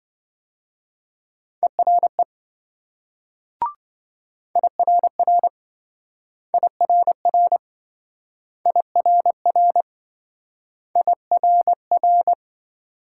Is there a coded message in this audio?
Yes: Morse code from 1.6 to 12.3 s, heard as a keyed tone, about 26 wpm, 700 Hz. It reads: ERE IRR IRR IRR IRR